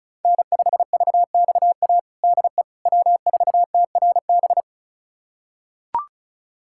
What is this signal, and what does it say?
Morse code, heard as a keyed tone, about 36 wpm, 700 Hz. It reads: N5VXA DE W4TRB
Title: USA Callsign DE Callsign - 5 Characters - 35wpm